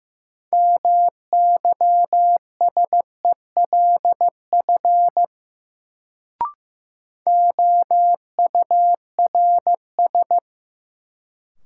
Morse code, heard as a keyed tone, about 15 wpm, 700 Hz. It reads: MYSELF OURS
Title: Sentences from Top 1300 Words - New Word Drill Alt - 15wpm